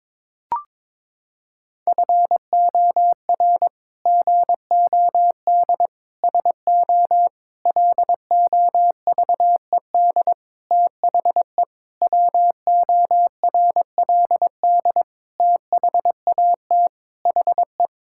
Morse code, heard as a keyed tone, about 22 wpm, 700 Hz. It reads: FOR GOD SO LOVED THE WORLD THAT HE
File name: practice / 22wpm / Sets of 8 Words - 22wpm